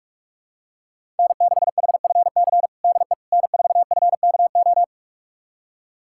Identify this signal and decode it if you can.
Morse code, heard as a keyed tone, about 46 wpm, 700 Hz. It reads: N6HFC DE N4FKO
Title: USA Callsign DE Callsign - 5 Characters - 45wpm